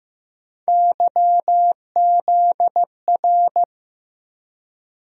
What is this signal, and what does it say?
Morse code, heard as a keyed tone, about 15 wpm, 700 Hz. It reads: YZR
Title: USA Call Sign Postfixes - 3 Letters 15wpm